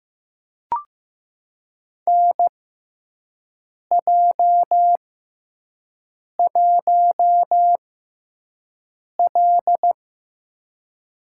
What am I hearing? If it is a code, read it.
Morse code, heard as a keyed tone, about 15 wpm, spaced as for 5 wpm, 700 Hz. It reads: NJ1L
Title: USA Call Signs - 4 Characters - 6x Character Spacing - 15wpm